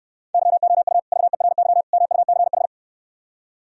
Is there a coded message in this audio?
Yes: Morse code from 0.3 to 2.7 s, heard as a keyed tone, about 67 wpm, 700 Hz. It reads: YOU VERY MUCH